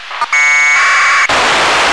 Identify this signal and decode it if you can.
Packet radio (AX.25, as in APRS).
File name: lu4aao / globo19 1 LU2DPW